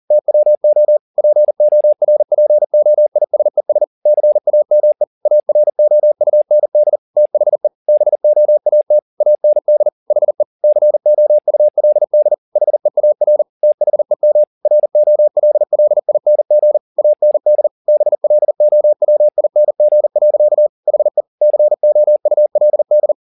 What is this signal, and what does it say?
Morse code, heard as a keyed tone, about 40 wpm, 600 Hz. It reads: TWO PORPOISES CAME AROUND THE BOAT AND HE COULD HEAR THEM ROLLING AND BLOWING. HE COULD